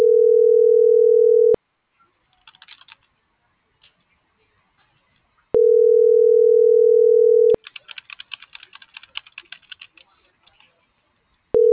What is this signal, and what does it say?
Ringback tone (North America), 440 + 480 Hz.